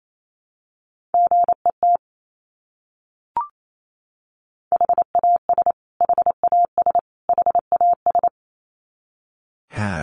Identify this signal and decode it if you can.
Morse code, heard as a keyed tone, about 28 wpm, 700 Hz. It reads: GET HAS HAS HAS
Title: Top 200 Words Repeated 28wpm